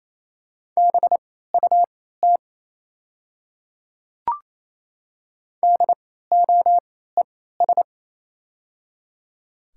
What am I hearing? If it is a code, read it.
Morse code, heard as a keyed tone, about 28 wpm, spaced as for 16 wpm, 700 Hz. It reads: BUT DOES
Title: 3x Character Spacing - Top 500 Words - 28wpm